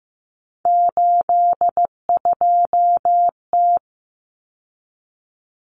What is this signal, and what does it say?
Morse code, heard as a keyed tone, about 15 wpm, 700 Hz. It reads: 82T
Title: Common Cut Numbers - 3-Digit 15wpm